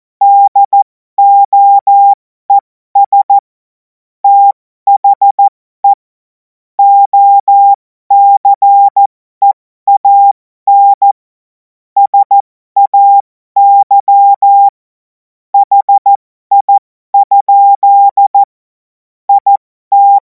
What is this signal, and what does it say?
Morse code, heard as a keyed tone, about 14 wpm, spaced as for 12 wpm, 800 Hz. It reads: DOES THE OCEAN SAY HI? IT
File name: Morsekurs / Witze